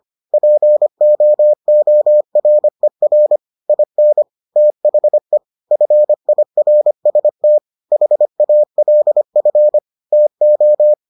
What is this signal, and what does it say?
Morse code, heard as a keyed tone, about 25 wpm, 600 Hz. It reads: POORER IN THE FIRST HALF TO